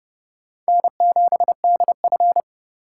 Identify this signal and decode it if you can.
Morse code, heard as a keyed tone, about 29 wpm, 700 Hz. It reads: N7DF